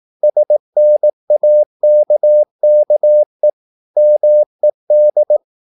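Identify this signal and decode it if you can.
Morse code, heard as a keyed tone, about 18 wpm, 600 Hz. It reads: SNAKKE MED
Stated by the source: Slow (QRS) Morse practice at 18 wpm, CW med Gnister.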